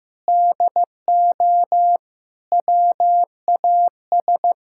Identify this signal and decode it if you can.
Morse code, heard as a keyed tone, about 15 wpm, 700 Hz. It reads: DO WAS
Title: Sentences from Top 1000 Words - Review Part 6 - 15wpm